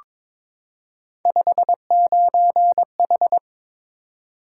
Morse code, heard as a keyed tone, about 22 wpm, 700 Hz. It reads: 59H
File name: practice / 22wpm / Three Letter-Number Combinations - 22wpm